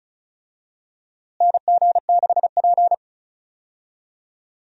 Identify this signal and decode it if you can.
Morse code, heard as a keyed tone, about 36 wpm, 700 Hz. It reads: NG6P
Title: USA Call Signs - 4 Characters - Repeated 3x - 35wpm